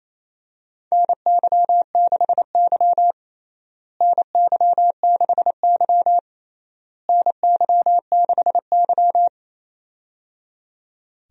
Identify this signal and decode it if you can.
Morse code, heard as a keyed tone, about 28 wpm, 700 Hz. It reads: NY6Y NY6Y NY6Y
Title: USA Call Signs - 4 Characters - Repeated 3x - 28wpm